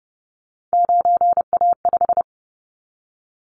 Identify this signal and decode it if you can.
Morse code, heard as a keyed tone, about 30 wpm, 700 Hz. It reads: 9A5
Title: Callsign Prefixes for Top 25 DXCC - 30wpm